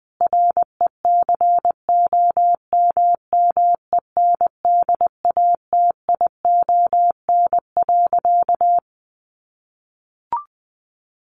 Morse code, heard as a keyed tone, about 20 wpm, 700 Hz. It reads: RECOMMENDATION.
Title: Sentences from Top 1400 Words 20wpm